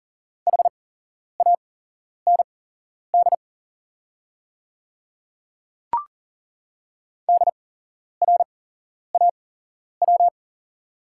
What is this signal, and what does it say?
Morse code, heard as a keyed tone, about 42 wpm, spaced as for 11 wpm, 690 Hz. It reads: HAND DRAW
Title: 8x Character Spacing - Top 300 Words - 40wpm